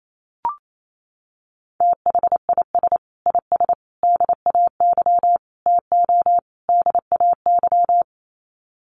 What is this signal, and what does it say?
Morse code, heard as a keyed tone, about 28 wpm, 700 Hz. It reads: THIS IS DAY TO DAY
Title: Sets of 5 Words Encore 28wpm